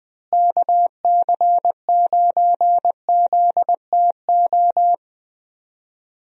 Morse code, heard as a keyed tone, about 20 wpm, 700 Hz. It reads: KC9ZTO
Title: USA Call Signs - 6 Characters - Repeated 3x - 20wpm